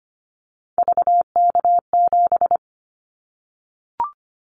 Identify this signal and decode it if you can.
Morse code, heard as a keyed tone, about 25 wpm, 700 Hz. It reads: VK7
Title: Callsign Prefixes for Top 100 DXCC - 25wpm